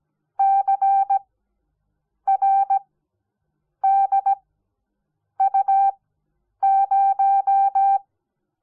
Morse code, heard as a keyed tone, about 17 wpm, spaced as for 7 wpm, 780 Hz. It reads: CRDU0
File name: Onlinelehrgang / morsen / DJ4UF Morsen05